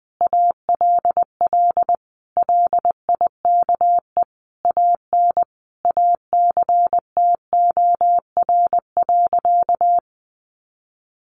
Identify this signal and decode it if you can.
Morse code, heard as a keyed tone, about 19 wpm, 700 Hz. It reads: ALL LIKE AN ACTOR.
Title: Sentences from Top 1500 Words 20wpm